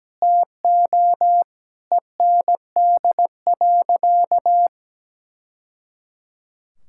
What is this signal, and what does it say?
Morse code, heard as a keyed tone, about 17 wpm, 700 Hz. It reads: TO END.
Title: Sentences from Top 1000 Words - Review Part 5 - 17wpm